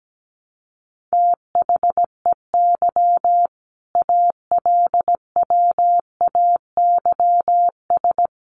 Morse code, heard as a keyed tone, about 17 wpm, 700 Hz. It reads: THEY ALWAYS
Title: Sentences from Top 300 Words 17wpm